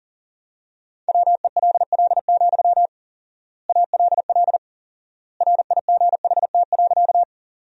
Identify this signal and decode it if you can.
Morse code, heard as a keyed tone, about 40 wpm, 700 Hz. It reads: WELL, ALL RIGHT.
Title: 4x Word-Spacing - Sentences from Top 200 Words - 40wpm